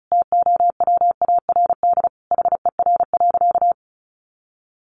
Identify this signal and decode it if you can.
Morse code, heard as a keyed tone, about 34 wpm, 700 Hz. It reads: TOWARD HER.
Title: Sentences from Top 1500 Words 35wpm